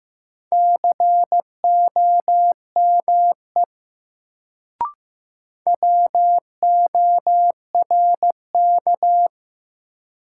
Morse code, heard as a keyed tone, about 15 wpm, 700 Hz. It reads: COME WORK
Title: Sets of 2 Words - New Word Drill - 15wpm